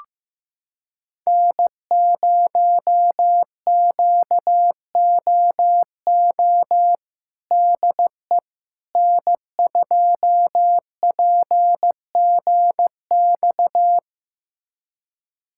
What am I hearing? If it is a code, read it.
Morse code, heard as a keyed tone, about 15 wpm, 700 Hz. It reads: N0QOO DE N2PGX
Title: USA Callsign DE Callsign - 5 Characters - 15wpm